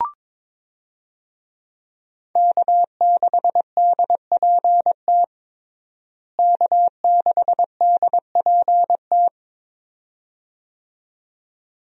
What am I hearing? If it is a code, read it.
Morse code, heard as a keyed tone, about 21 wpm, 700 Hz. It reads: K6DPT K6DPT
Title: USA Call Signs - 5 Characters - Repeated 2x - 22wpm